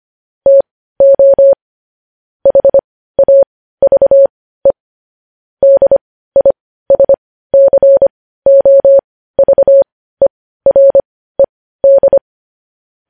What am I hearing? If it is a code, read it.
Morse code, heard as a keyed tone, about 25 wpm, spaced as for 15 wpm, 550 Hz. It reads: TO HAVE DISCOVERED
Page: news-headlines-in-morse-code-at-15-wpm